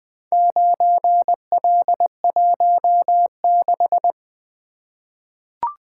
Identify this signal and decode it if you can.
Morse code, heard as a keyed tone, about 20 wpm, 700 Hz. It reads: 9L16
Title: Four Letter-Number Combinations - 20wpm